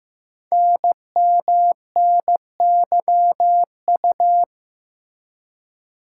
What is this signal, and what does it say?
Morse code, heard as a keyed tone, about 15 wpm, 700 Hz. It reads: NMNYU
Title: Code Groups - 5 Letters - 15wpm